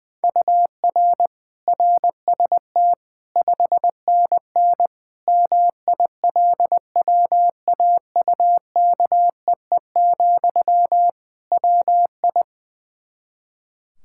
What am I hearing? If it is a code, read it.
Morse code, heard as a keyed tone, about 20 wpm, 700 Hz. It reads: UR RST 5NN MILWAUKEE, WI
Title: K1USN SST QSO Practice 20wpm